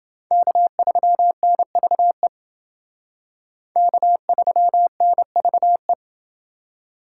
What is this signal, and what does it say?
Morse code, heard as a keyed tone, about 29 wpm, 700 Hz. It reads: K3NVE K3NVE